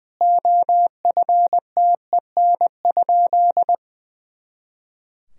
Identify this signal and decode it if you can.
Morse code, heard as a keyed tone, about 20 wpm, 700 Hz. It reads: OFTEN?
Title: Sentences from Top 400 Words and Abbreviations 20wpm